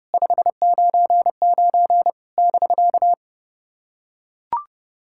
Morse code, heard as a keyed tone, about 30 wpm, 700 Hz. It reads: 599 <BK>
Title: SOTA QSO Practice 30wpm